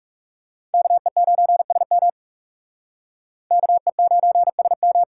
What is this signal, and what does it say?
Morse code, heard as a keyed tone, about 39 wpm, 700 Hz. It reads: KE9SM KE9SM